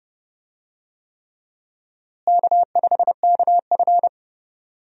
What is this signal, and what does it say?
Morse code, heard as a keyed tone, about 30 wpm, 700 Hz. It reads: K5KF
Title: USA Call Signs - 4 Characters - Repeated 3x - 30wpm